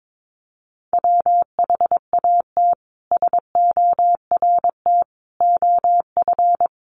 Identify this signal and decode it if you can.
Morse code, heard as a keyed tone, about 22 wpm, 700 Hz. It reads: WHAT SORT OF
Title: A Sound Of Thunder 22wpm